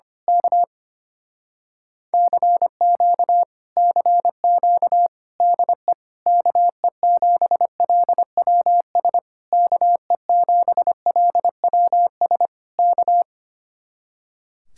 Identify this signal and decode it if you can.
Morse code, heard as a keyed tone, about 24 wpm, 700 Hz. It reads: K CQ CQ DE KE7LWS KE7LWS K
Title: Speed-Racing Common QSO Phrases - 25wpm